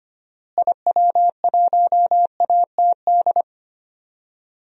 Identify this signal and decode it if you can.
Morse code, heard as a keyed tone, about 25 wpm, 700 Hz. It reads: IW1ATD
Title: DXCC-100 Call Signs - 6 Characters - Repeated 3x - 25wpm